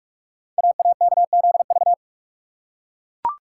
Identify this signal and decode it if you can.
Morse code, heard as a keyed tone, about 44 wpm, 700 Hz. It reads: AAKZV
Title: Code Groups - 5 Letter:Numbers - 45wpm